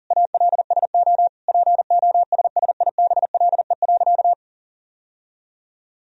Morse code, heard as a keyed tone, about 40 wpm, 700 Hz. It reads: ALSO POSSIBLE.
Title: Sentences from Top 1000 Words - Review Part 1 - 40wpm